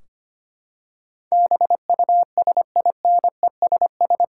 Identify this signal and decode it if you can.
Morse code, heard as a keyed tone, about 25 wpm, 700 Hz. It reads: BUSINESS
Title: Sentences from Top 1600 Words 25wpm